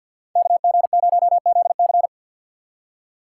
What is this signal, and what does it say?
Morse code, heard as a keyed tone, about 53 wpm, 700 Hz. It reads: KG0ZC
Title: USA Call Signs - 5 Characters - 6x Character Spacing - 50wpm